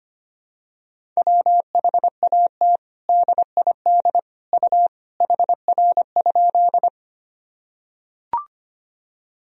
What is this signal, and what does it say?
Morse code, heard as a keyed tone, about 25 wpm, 700 Hz. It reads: WHAT DID U HR?
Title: Sentences from Top 300 Words and Abbreviations 25wpm